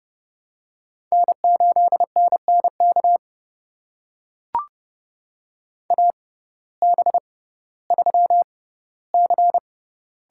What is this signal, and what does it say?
Morse code, heard as a keyed tone, about 30 wpm, spaced as for 16 wpm, 700 Hz. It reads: N8NNK AB3C
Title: USA Call Signs - 5 Characters - 6x Character Spacing - 30wpm